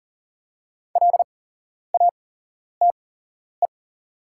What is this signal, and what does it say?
Morse code, heard as a keyed tone, about 40 wpm, spaced as for 11 wpm, 690 Hz. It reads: LATE
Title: 8x Character Spacing - Top 500 Words - 40wpm